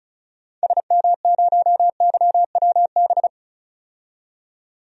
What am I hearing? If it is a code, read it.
Morse code, heard as a keyed tone, about 34 wpm, 700 Hz. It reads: SM0YWB
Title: DXCC-100 Call Signs - 6 Characters - 35wpm